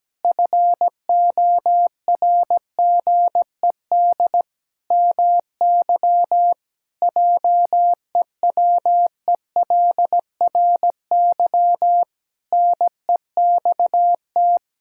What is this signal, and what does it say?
Morse code, heard as a keyed tone, about 17 wpm, 700 Hz. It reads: FORGED MY JEWELRY NEXT
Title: Pangram Challenge v2 - 17wpm